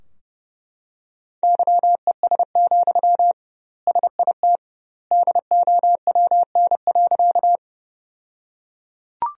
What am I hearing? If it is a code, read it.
Morse code, heard as a keyed tone, about 32 wpm, 700 Hz. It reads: YES, SIT DOWN.
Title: 2x Word-Spacing - Sentences from Top 500 Words - 30wpm